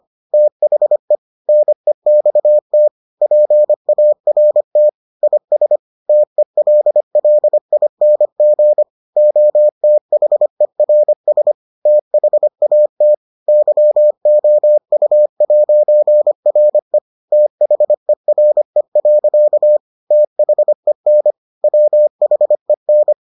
Morse code, heard as a keyed tone, about 25 wpm, 600 Hz. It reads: THE NEXT PART IS TELLING OTHERS THAT YOU'RE THERE. THEN WHEN